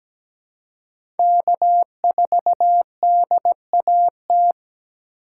Morse code, heard as a keyed tone, about 17 wpm, 700 Hz. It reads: K4DAT
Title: USA Call Signs - 4x Character Spacing - 17wpm